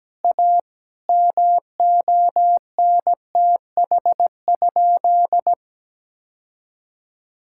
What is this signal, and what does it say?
Morse code, heard as a keyed tone, about 17 wpm, 700 Hz. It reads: A MONTH?
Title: Sentences from Top 1000 Words - Review Part 6 - 17wpm